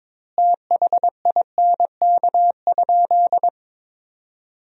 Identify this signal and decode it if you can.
Morse code, heard as a keyed tone, about 22 wpm, 700 Hz. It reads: THINK?
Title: 2x Word-Spacing - Sentences from Top 200 Words - 22wpm